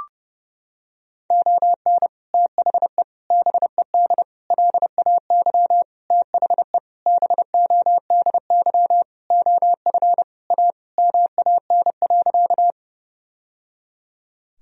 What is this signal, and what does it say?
Morse code, heard as a keyed tone, about 30 wpm, 700 Hz. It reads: ON THE BED LAY THE BODY OF A MAN.
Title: Sentences from Top 1000 Words - Review Part 2 - 30wpm